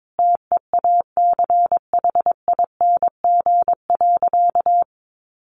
Morse code, heard as a keyed tone, about 22 wpm, 700 Hz. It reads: TEACHING.
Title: Sentences from Top 1100 Words 22wpm